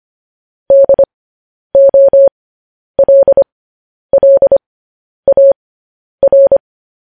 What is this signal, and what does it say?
Morse code, heard as a keyed tone, about 25 wpm, spaced as for 10 wpm, 550 Hz. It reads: DOLLAR